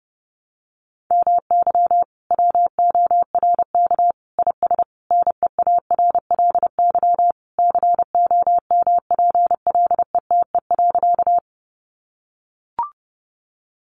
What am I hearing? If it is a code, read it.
Morse code, heard as a keyed tone, about 30 wpm, 700 Hz. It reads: MY WORK IS NEARLY COMPLETE.
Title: Sentences from Top 1300 Words 30wpm